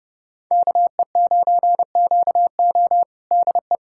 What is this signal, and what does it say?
Morse code, heard as a keyed tone, about 30 wpm, 700 Hz. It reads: KE9QO DE
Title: USA Callsign DE Callsign - 5 Characters - 30wpm